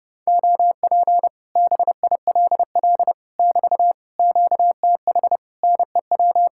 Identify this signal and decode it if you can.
Morse code, heard as a keyed tone, about 29 wpm, 700 Hz. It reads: OP BILL <BT> QTH NEW